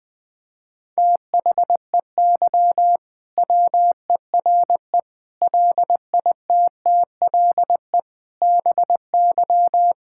Morse code, heard as a keyed tone, about 20 wpm, 700 Hz. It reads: THEY WERE LITTLE BY
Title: Sentences from Top 1600 Words 20wpm